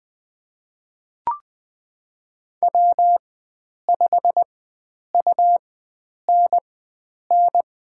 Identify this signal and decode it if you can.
Morse code, heard as a keyed tone, about 20 wpm, spaced as for 9 wpm, 700 Hz. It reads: W5UNN
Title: USA Call Signs - 5 Characters - 4x Character Spacing - 20wpm